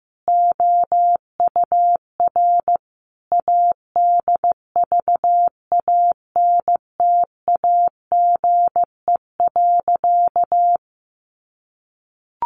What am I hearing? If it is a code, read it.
Morse code, heard as a keyed tone, about 15 wpm, 700 Hz. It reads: OUR ADVANTAGE.
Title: Sentences from Top 1400 Words 15wpm